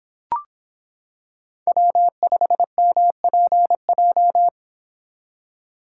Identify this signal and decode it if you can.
Morse code, heard as a keyed tone, about 25 wpm, 700 Hz. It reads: W5MPJ